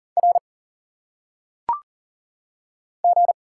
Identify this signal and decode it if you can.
Morse code, heard as a keyed tone, about 40 wpm, 700 Hz. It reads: R G
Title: Single Letters - Phonetic Pronunciation 40wpm